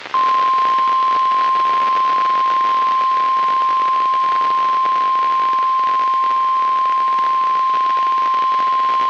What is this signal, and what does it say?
Emergency Alert System (SAME) tones (the 1050 Hz attention tone), with noise. It is heard from 1.1 to 8.6 s.
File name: eas messages / EAS WXL38 Binghamton